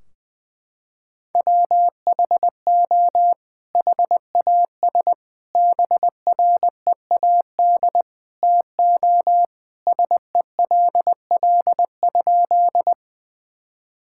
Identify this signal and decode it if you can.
Morse code, heard as a keyed tone, about 20 wpm, 700 Hz. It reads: WHO HAS BREAD TO SELL?
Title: Sentences from Top 1600 Words 20wpm